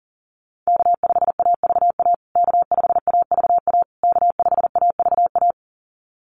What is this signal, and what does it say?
Morse code, heard as a keyed tone, about 40 wpm, 700 Hz. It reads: K5AVA K5AVA K5AVA
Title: USA Call Signs Repeated 3x 40wpm